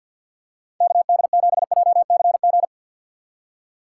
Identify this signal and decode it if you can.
Morse code, heard as a keyed tone, about 53 wpm, 700 Hz. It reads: KD7JCG
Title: USA Call Signs - 2x Character Spacing - 50wpm